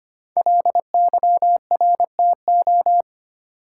Morse code, heard as a keyed tone, about 25 wpm, 700 Hz. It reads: LYRTO
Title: Code Groups - 5 Letters - 25wpm